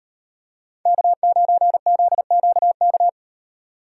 Morse code, heard as a keyed tone, about 38 wpm, 700 Hz. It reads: K9ZQK